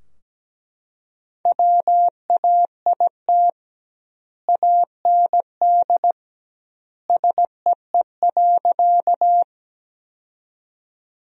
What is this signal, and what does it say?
Morse code, heard as a keyed tone, about 17 wpm, 700 Hz. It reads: WAIT AND SEE.
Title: 2x Word-Spacing - Sentences from Top 500 Words - 17wpm